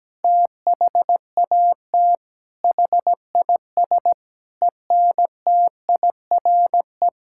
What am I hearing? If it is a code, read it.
Morse code, heard as a keyed tone, about 17 wpm, 700 Hz. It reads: THAT HIS ENTIRE
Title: Sentences from Top 1000 Words - Review Part 5 - 17wpm